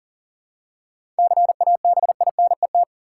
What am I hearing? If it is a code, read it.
Morse code, heard as a keyed tone, about 40 wpm, 700 Hz. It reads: CABINET
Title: Seven Letter Words - 40wpm